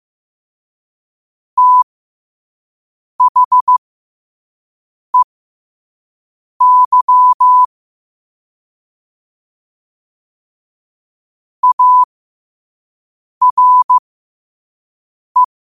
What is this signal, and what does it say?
Morse code, heard as a keyed tone, about 15 wpm, spaced as for 5 wpm, 1000 Hz. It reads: THEY ARE
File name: qotdfiles / 5wpm